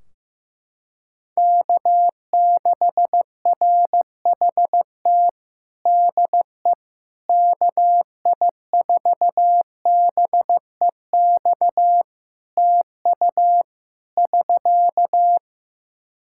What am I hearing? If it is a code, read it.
Morse code, heard as a keyed tone, about 15 wpm, 700 Hz. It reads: K6RHT DE KI4BEX TU <SK>
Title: K1USN SST QSO Practice 15wpm